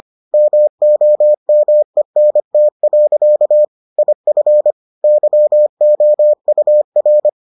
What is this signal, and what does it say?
Morse code, heard as a keyed tone, about 25 wpm, 600 Hz. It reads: MOMENT. IF YOUR